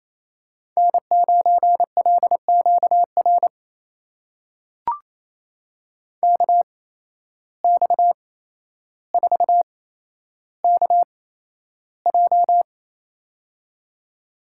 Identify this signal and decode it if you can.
Morse code, heard as a keyed tone, about 28 wpm, spaced as for 12 wpm, 700 Hz. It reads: N9LQR KX4KJ
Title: USA Call Signs - 5 Characters - 8x Character Spacing - 28wpm